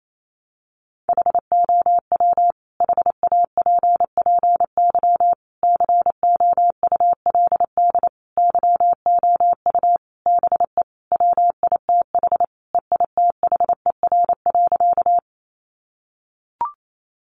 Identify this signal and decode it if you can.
Morse code, heard as a keyed tone, about 28 wpm, 700 Hz. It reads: HOW HAPPY COULD YOU BE WITH EITHER.
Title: Sentences from Top 1000 Words 28wpm